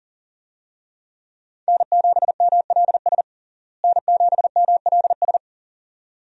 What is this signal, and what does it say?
Morse code, heard as a keyed tone, about 37 wpm, 700 Hz. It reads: N7MLS N7MLS
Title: USA Call Signs - 5 Characters - Repeated 2x - 40wpm